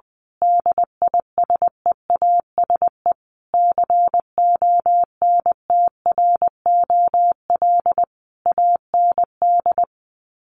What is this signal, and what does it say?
Morse code, heard as a keyed tone, about 20 wpm, 700 Hz. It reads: DISEASE CONTROL AND
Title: Sets of 5 Words Encore2 20wpm